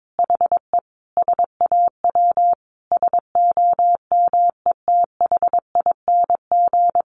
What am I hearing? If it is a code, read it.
Morse code, heard as a keyed tone, about 22 wpm, 700 Hz. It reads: HE SAW SOMETHING
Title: Sentences from Top 1200 Words 22wpm